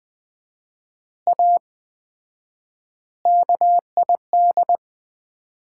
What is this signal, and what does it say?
Morse code, heard as a keyed tone, about 20 wpm, 700 Hz. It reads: A KID
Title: 4x Word-Spacing - Sets of 5 Words - 20wpm